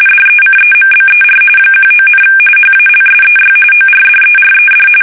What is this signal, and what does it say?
RTTY, about 45.45 baud. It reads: DECEMBER 19, HALSEY ORDERED
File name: fsk850 / battleshipnj